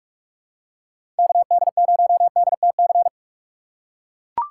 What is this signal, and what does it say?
Morse code, heard as a keyed tone, about 44 wpm, 690 Hz. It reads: KD0DTC
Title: USA Call Signs - 6 Characters - 3x Character Spacing - 45wpm